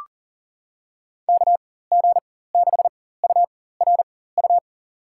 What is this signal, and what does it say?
Morse code, heard as a keyed tone, about 37 wpm, spaced as for 18 wpm, 700 Hz. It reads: KG6URU
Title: USA Call Signs - 4x Character Spacing - 40wpm